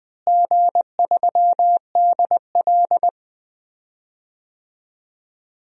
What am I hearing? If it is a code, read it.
Morse code, heard as a keyed tone, about 20 wpm, 700 Hz. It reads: G3DL
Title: DXCC-100 Call Signs - 4 Characters - Repeated 3x - 20wpm